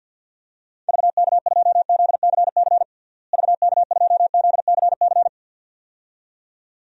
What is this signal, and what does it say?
Morse code, heard as a keyed tone, about 52 wpm, 700 Hz. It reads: VK2ZCC VK2ZCC
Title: DXCC-100 Call Signs - 6 Characters - Repeated 2x - 50wpm